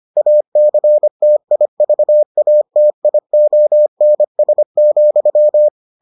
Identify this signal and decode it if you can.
Morse code, heard as a keyed tone, about 25 wpm, 600 Hz. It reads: ACTIVATIONS,